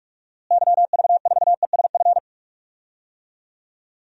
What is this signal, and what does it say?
Morse code, heard as a keyed tone, about 42 wpm, 700 Hz. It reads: YV4ESF
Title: DXCC-100 Call Signs - 6 Characters - Repeated 3x - 45wpm